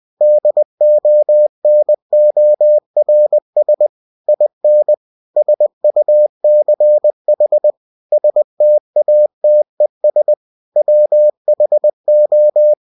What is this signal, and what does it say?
Morse code, heard as a keyed tone, about 20 wpm, 600 Hz. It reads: DONORS IN SUCH STATES WHO